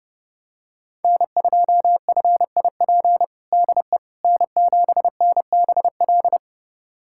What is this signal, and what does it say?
Morse code, heard as a keyed tone, about 30 wpm, 700 Hz. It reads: N2FIP DE N7NBL
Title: USA Callsign DE Callsign - 5 Characters - 30wpm